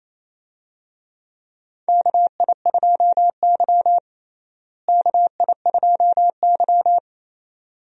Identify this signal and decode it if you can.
Morse code, heard as a keyed tone, about 28 wpm, 700 Hz. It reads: KI2Y KI2Y
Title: USA Call Signs - 4 Characters - Repeated 2x - 28wpm